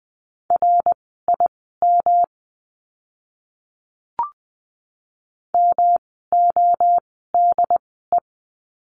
Morse code, heard as a keyed tone, about 20 wpm, spaced as for 14 wpm, 700 Hz. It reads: RIM MODE
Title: 108 Lesson 18f - Letter m - Words 20wpm